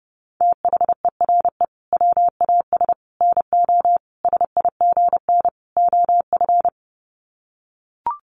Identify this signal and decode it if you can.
Morse code, heard as a keyed tone, about 28 wpm, 700 Hz. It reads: THERE WAS NO SIGN OF
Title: Sets of 5 Words Encore2 30wpm